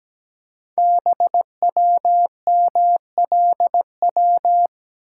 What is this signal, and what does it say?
Morse code, heard as a keyed tone, about 17 wpm, 700 Hz. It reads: BWMLW
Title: Code Groups - 5 Letters - 17wpm